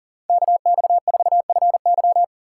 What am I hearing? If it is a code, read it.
Morse code, heard as a keyed tone, about 40 wpm, 700 Hz. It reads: KX4FY
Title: USA Call Signs - 5 Characters - 3x Character Spacing - 40wpm